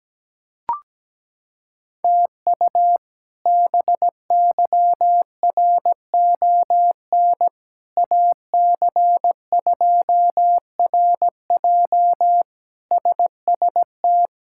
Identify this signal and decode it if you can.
Morse code, heard as a keyed tone, about 17 wpm, 700 Hz. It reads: TU BYRON AC2RJ SST
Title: K1USN SST QSO Practice - 17wpm